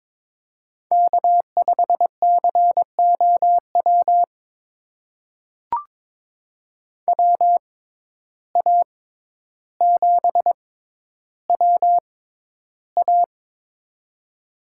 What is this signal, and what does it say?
Morse code, heard as a keyed tone, about 22 wpm, spaced as for 11 wpm, 700 Hz. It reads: K5COW WA7WA
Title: USA Call Signs - 5 Characters - 6x Character Spacing - 22wpm